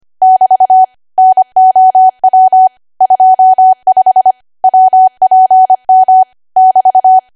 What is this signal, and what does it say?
Morse code, heard as a keyed tone, about 25 wpm, 750 Hz. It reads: <BT> NOW 25 WPM <BT>
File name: cwa / W1AW-Level3 / 25WPM